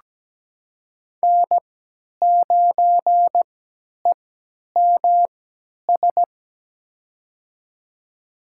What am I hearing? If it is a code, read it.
Morse code, heard as a keyed tone, about 17 wpm, spaced as for 10 wpm, 700 Hz. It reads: N9EMS
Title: USA Call Signs - 5 Characters - 3x Character Spacing - 17wpm